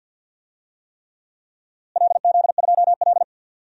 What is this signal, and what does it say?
Morse code, heard as a keyed tone, about 49 wpm, 700 Hz. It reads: LZ2L
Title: DXCC-100 Call Signs - 4 Characters - Repeated 3x - 50wpm